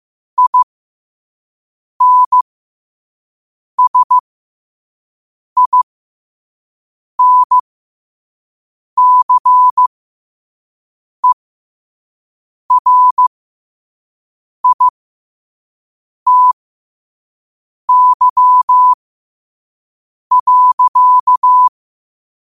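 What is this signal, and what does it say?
Morse code, heard as a keyed tone, about 15 wpm, spaced as for 5 wpm, 1000 Hz. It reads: INSINCERITY.
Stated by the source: Morse code sent at 5 wpm.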